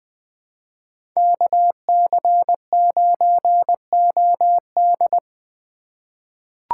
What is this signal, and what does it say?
Morse code, heard as a keyed tone, about 20 wpm, 700 Hz. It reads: KC9OD
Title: USA Call Signs - 5 Characters - 4x Character Spacing - 20wpm